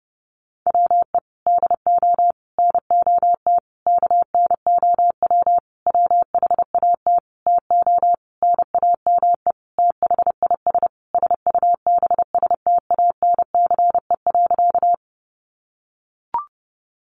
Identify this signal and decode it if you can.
Morse code, heard as a keyed tone, about 30 wpm, 700 Hz. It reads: WE DO NOT KNOW WHAT TO NAME THIS SUBSTANCE.
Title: Sentences from Top 1500 Words 30wpm